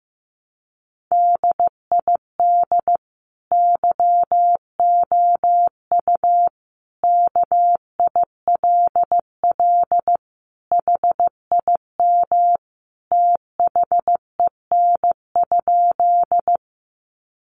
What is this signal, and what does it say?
Morse code, heard as a keyed tone, about 14 wpm, 700 Hz. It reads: DID YOU KILL HIM THEN?
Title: Sentences from Top 1400 Words 15wpm